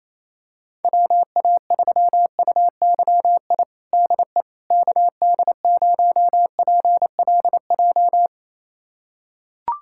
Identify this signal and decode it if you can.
Morse code, heard as a keyed tone, about 27 wpm, 700 Hz. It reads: WA3UYI DE KD0PLJ
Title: USA Callsign DE Callsign - 6 Characters - 28wpm